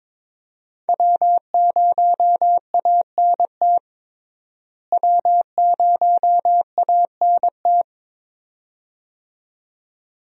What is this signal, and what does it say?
Morse code, heard as a keyed tone, about 22 wpm, 700 Hz. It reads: W0ANT W0ANT
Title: USA Call Signs - 5 Characters - Repeated 2x - 22wpm